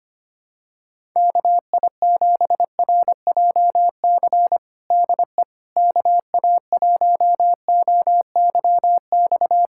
Morse code, heard as a keyed tone, about 25 wpm, 700 Hz. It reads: KI7RJC DE KA1OYX
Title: USA Callsign DE Callsign - 6 Characters - 25wpm